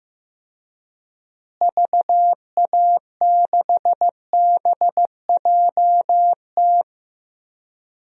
Morse code, heard as a keyed tone, about 15 wpm, 700 Hz. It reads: VA6BJT
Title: DXCC-100 Call Signs - 6 Characters - Repeated 2x - 15wpm